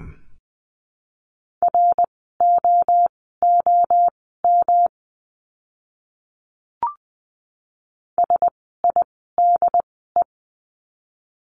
Morse code, heard as a keyed tone, about 20 wpm, spaced as for 14 wpm, 700 Hz. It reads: ROOM SIDE
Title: 115 Lesson 19f - Letter w - Words 20wpm